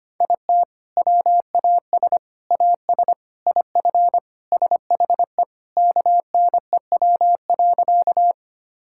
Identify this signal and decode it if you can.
Morse code, heard as a keyed tone, about 25 wpm, 700 Hz. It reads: IT WAS AS IF SHE KNEW.
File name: practice / 25wpm / Sentences with 4-Word Phrases - 25wpm